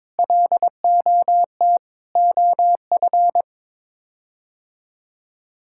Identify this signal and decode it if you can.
Morse code, heard as a keyed tone, about 22 wpm, 700 Hz. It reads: LOT OF
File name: practice / 22wpm / Sets of 8 Words - 22wpm